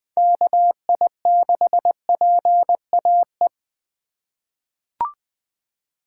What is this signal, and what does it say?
Morse code, heard as a keyed tone, about 20 wpm, 700 Hz. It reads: KI6PAE
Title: USA Call Signs - 6 Characters - 2x Character Spacing - 20wpm